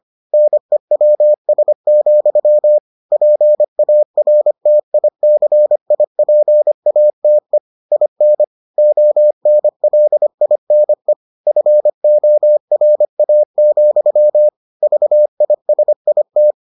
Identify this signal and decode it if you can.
Morse code, heard as a keyed tone, about 25 wpm, 600 Hz. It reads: NEWS, PARTICIPATE IN ONLINE FORA, VISIT